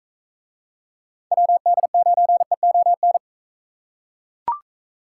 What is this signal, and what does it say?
Morse code, heard as a keyed tone, about 39 wpm, 700 Hz. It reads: WD9EON